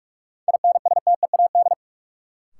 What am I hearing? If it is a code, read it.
Morse code, heard as a keyed tone, about 42 wpm, 700 Hz. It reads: INSTEAD
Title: Sentences from Top 800 Words - New Word Drill - 45wpm